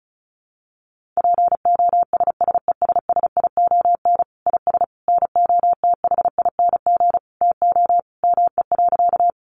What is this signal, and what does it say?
Morse code, heard as a keyed tone, about 34 wpm, 700 Hz. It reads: POSSESSION IS NOTHING TO ME.
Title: Sentences from Top 1400 Words 35wpm